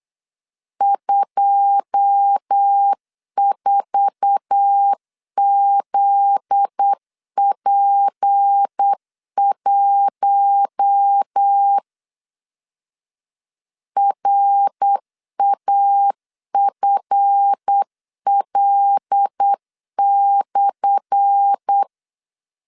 Morse code, heard as a keyed tone, about 8 wpm, 800 Hz. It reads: 24ZP1 RAFL/
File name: LEZIONI / 10 SIGNS / 10A SLASH 8wpm 3452